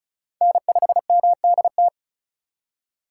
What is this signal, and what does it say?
Morse code, heard as a keyed tone, about 36 wpm, 700 Hz. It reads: N5MDT
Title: USA Call Signs - 3x Character Spacing - 35wpm